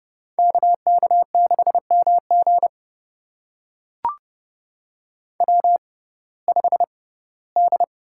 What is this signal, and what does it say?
Morse code, heard as a keyed tone, about 29 wpm, spaced as for 18 wpm, 700 Hz. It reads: KK6MG W5D
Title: USA Call Signs - 5 Characters - 6x Character Spacing - 30wpm